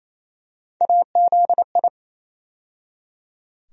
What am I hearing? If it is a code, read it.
Morse code, heard as a keyed tone, about 28 wpm, 700 Hz. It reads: AZI
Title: USA Call Sign Postfixes - 3 Letters 28wpm